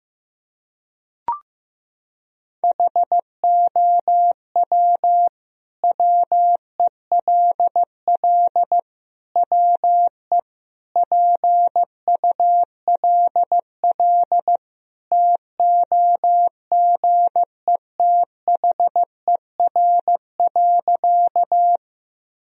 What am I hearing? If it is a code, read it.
Morse code, heard as a keyed tone, about 15 wpm, 700 Hz. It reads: HOW WELL WE PULL TOGETHER.
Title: Sentences from Top 1000 Words - Review Part 2 - 15wpm